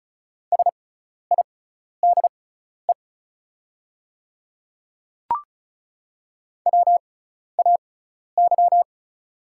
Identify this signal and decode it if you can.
Morse code, heard as a keyed tone, about 34 wpm, spaced as for 12 wpm, 690 Hz. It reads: SIDE WAY
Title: 6x Character Spacing - Top 100 Words - 35wpm